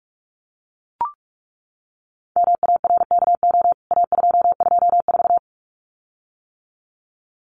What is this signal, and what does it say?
Morse code, heard as a keyed tone, about 44 wpm, 700 Hz. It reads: MARKO A224
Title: CWT Exchange 45wpm